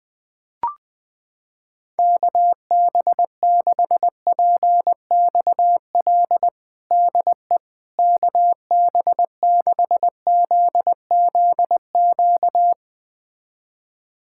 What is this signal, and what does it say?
Morse code, heard as a keyed tone, about 20 wpm, 700 Hz. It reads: KB6PXL DE KB6ZZQ
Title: USA Callsign DE Callsign - 6 Characters - 20wpm